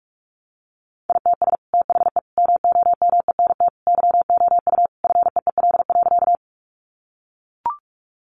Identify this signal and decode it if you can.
Morse code, heard as a keyed tone, about 44 wpm, 700 Hz. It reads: ITS THE MOMENT YOU FEEL.
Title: Sentences from Top 1000 Words 45wpm